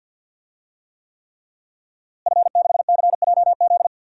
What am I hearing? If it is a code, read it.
Morse code, heard as a keyed tone, about 50 wpm, 700 Hz. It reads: F6CJZ
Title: DXCC-100 Call Signs - 5 Characters - Repeated 3x - 50wpm